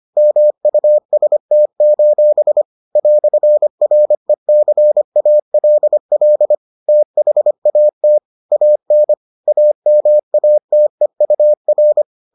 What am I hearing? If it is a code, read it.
Morse code, heard as a keyed tone, about 25 wpm, 600 Hz. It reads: MUST: "RECALL THAT AN AMATEUR